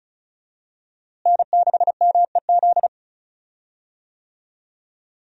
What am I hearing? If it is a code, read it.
Morse code, heard as a keyed tone, about 34 wpm, 700 Hz. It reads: N6MEZ
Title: USA Call Signs - 5 Characters - Repeated 2x - 35wpm